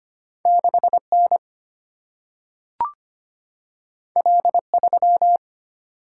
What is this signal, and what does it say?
Morse code, heard as a keyed tone, about 25 wpm, 700 Hz. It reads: 6N L3
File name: Two Letter-Number Combinations - 25wpm